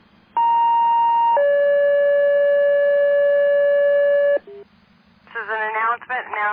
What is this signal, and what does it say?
Two-tone page, heard as a one-second tone, then a three-second tone, about 930 then 570 Hz, with speech. It is heard from 0.4 to 4.4 s.